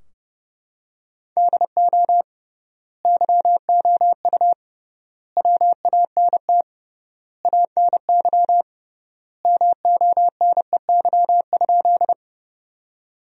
Morse code, heard as a keyed tone, about 29 wpm, 700 Hz. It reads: DO YOU WANT ANY MONEY?
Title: 3x Word-Spacing - Sentences from Top 400 Words - 30wpm